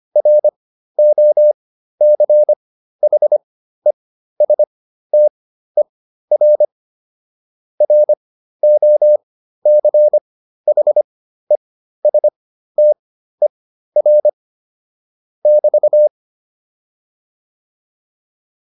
Morse code, heard as a keyed tone, about 25 wpm, spaced as for 13 wpm, 600 Hz. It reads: ROCHESTER ROCHESTER <BT>